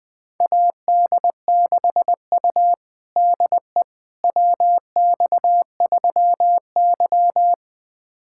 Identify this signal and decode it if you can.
Morse code, heard as a keyed tone, about 20 wpm, 700 Hz. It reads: AD6U DE WX3Y
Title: USA Callsign DE Callsign - 4 Characters - 20wpm